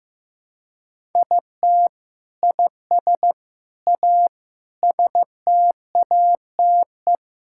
Morse code, heard as a keyed tone, about 16 wpm, 700 Hz. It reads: IT IS A STATE
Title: Sentences from Top 1000 Words - Review Part 5 - 15wpm